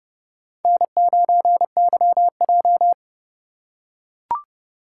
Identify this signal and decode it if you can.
Morse code, heard as a keyed tone, about 32 wpm, 700 Hz. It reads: N9YJ
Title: USA Call Signs - 4 Characters - 3x Character Spacing - 30wpm